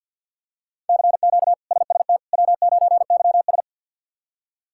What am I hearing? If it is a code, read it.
Morse code, heard as a keyed tone, about 47 wpm, 700 Hz. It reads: CQ SST W9YS
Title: K1USN SST QSO Practice - 50wpm